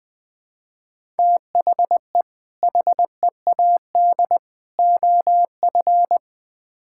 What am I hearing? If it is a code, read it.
Morse code, heard as a keyed tone, about 20 wpm, 700 Hz. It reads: THE HEAD OF
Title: Build A Phrase - 4-Word Phrases - 20wpm